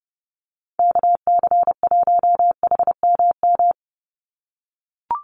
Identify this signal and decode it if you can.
Morse code, heard as a keyed tone, about 30 wpm, 700 Hz. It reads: KC1HMM
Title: USA Call Signs 30wpm